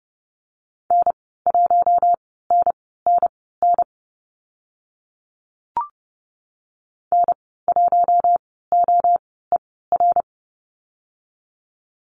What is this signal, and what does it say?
Morse code, heard as a keyed tone, about 30 wpm, spaced as for 17 wpm, 700 Hz. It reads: N1NNN N1OER
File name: Lesson 14g - Number 2 - Call Signs 30wpm